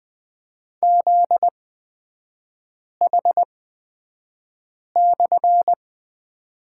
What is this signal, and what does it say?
Morse code, heard as a keyed tone, about 20 wpm, spaced as for 5 wpm, 700 Hz. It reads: ZH/
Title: 250 Lesson 40e - BK - Three Characters 20wpm